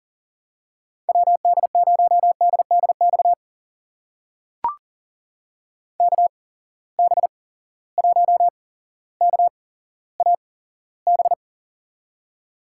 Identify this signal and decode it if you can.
Morse code, heard as a keyed tone, about 40 wpm, spaced as for 17 wpm, 700 Hz. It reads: WD0DDX KB1KAB
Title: USA Call Signs - 6 Characters - 8x Character Spacing - 40wpm